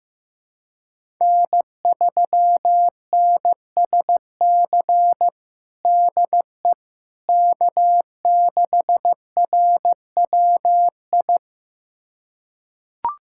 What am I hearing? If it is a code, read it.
Morse code, heard as a keyed tone, about 15 wpm, 700 Hz. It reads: N3NSC DE K6RWI
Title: USA Callsign DE Callsign - 5 Characters - 15wpm